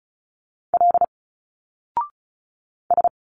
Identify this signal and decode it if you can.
Morse code, heard as a keyed tone, about 36 wpm, 700 Hz. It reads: L S
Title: Single Letters 35wpm